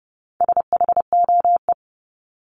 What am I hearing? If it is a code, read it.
Morse code, heard as a keyed tone, about 30 wpm, 700 Hz. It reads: SHOE
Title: Top 1751-2000 Words 30wpm